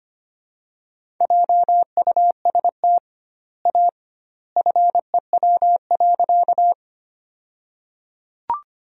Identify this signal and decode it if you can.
Morse code, heard as a keyed tone, about 25 wpm, 700 Hz. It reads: JUST A FEW.
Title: 2x Word-Spacing - Sentences from Top 300 Words - 25wpm